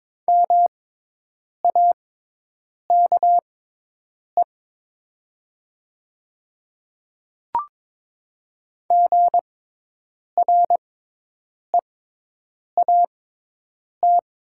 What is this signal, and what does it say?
Morse code, heard as a keyed tone, about 21 wpm, spaced as for 8 wpm, 700 Hz. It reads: MAKE GREAT
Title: 6x Character Spacing - Top 300 Words - 22wpm